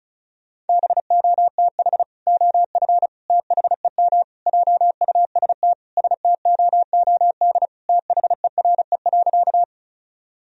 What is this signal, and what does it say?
Morse code, heard as a keyed tone, about 36 wpm, 700 Hz. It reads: BOTH OF THEM JUST STOOD THERE.
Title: Sentences from Top 1000 Words - Review Part 2 - 35wpm